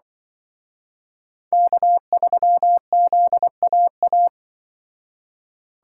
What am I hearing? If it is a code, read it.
Morse code, heard as a keyed tone, about 24 wpm, 700 Hz. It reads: K3ZAA
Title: Speed Racing - USA 1x3 Call Signs 20wpm